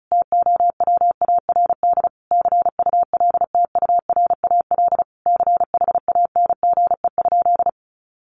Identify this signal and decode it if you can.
Morse code, heard as a keyed tone, about 34 wpm, 700 Hz. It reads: TOWARD CULTURAL CHANGE?
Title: Sentences from Top 1500 Words 35wpm